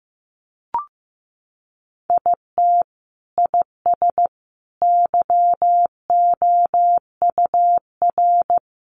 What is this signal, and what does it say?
Morse code, heard as a keyed tone, about 14 wpm, 700 Hz. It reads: IT IS YOUR
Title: Sentences from Top 1100 Words 15wpm